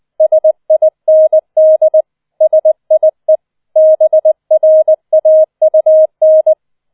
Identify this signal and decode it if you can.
Morse code, heard as a keyed tone, about 20 wpm, 630 Hz. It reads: SIND SIE BRAUN